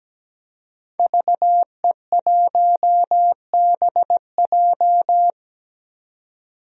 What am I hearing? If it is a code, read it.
Morse code, heard as a keyed tone, about 17 wpm, 700 Hz. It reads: VE1BJ
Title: DXCC-100 Call Signs - 5 Characters - 17wpm